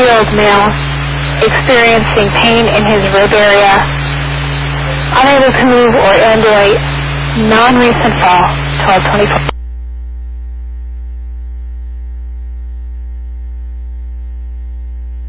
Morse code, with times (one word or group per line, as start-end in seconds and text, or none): none